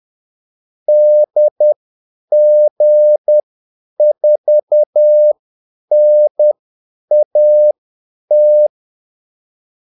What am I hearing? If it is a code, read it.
Morse code, heard as a keyed tone, about 10 wpm, spaced as for 8 wpm, 600 Hz. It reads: DG4NAT
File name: Callsigns-10-8wpm